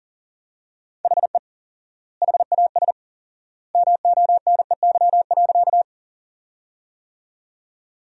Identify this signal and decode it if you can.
Morse code, heard as a keyed tone, about 37 wpm, 700 Hz. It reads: HE HAS MONEY.
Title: 4x Word-Spacing - Sentences from Top 400 Words - 40wpm